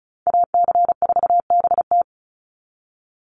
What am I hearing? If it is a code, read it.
Morse code, heard as a keyed tone, about 34 wpm, 700 Hz. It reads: AC4BT
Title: Call Signs - 35wpm